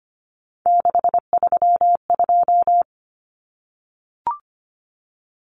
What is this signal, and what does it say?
Morse code, heard as a keyed tone, about 25 wpm, 700 Hz. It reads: 632
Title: Numbers - 3-Digit 25wpm